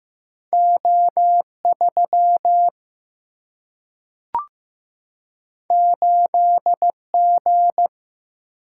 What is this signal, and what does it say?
Morse code, heard as a keyed tone, about 15 wpm, 700 Hz. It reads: O 3 8 G
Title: Single-Letter Single-Number 15wpm